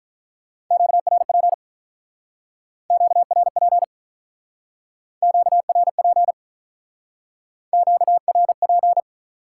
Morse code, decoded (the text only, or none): QRP QRP QRP QRP